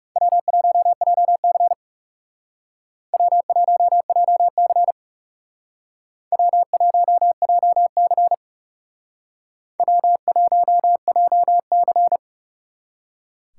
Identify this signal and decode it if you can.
Morse code, heard as a keyed tone, about 38 wpm, 700 Hz. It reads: W1JC W1JC W1JC W1JC